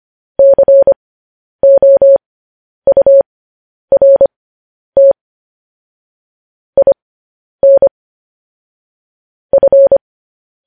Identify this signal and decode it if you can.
Morse code, heard as a keyed tone, about 25 wpm, spaced as for 10 wpm, 550 Hz. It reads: COURT IN F